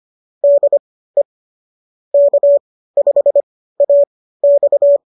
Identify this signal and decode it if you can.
Morse code, heard as a keyed tone, about 25 wpm, spaced as for 15 wpm, 590 Hz. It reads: DE K5AX
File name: qso201 15